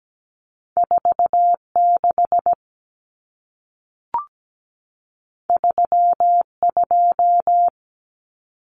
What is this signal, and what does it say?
Morse code, heard as a keyed tone, about 17 wpm, 700 Hz. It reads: 46 32
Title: Numbers - 2-Digit 17wpm